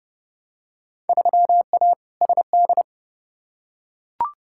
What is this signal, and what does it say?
Morse code, heard as a keyed tone, about 31 wpm, 700 Hz. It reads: 3A SD
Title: Field Day Exchanges - 30wpm